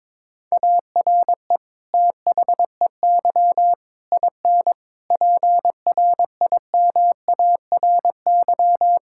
Morse code, heard as a keyed tone, about 22 wpm, 700 Hz. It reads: ARE THEY IN PRIMARY
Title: Sentences from Top 1000 Words - Review Part 5 - 22wpm